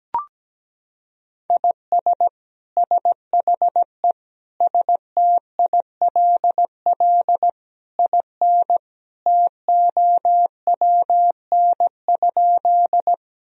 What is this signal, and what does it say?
Morse code, heard as a keyed tone, about 17 wpm, 700 Hz. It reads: IS SHE STILL IN TOWN?
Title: Sentences from Top 1000 Words - Review Part 2 - 17wpm